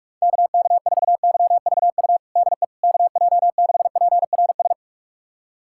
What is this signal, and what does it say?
Morse code, heard as a keyed tone, about 46 wpm, 700 Hz. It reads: KK4YVU DE KJ6PRS
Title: USA Callsign DE Callsign 45wpm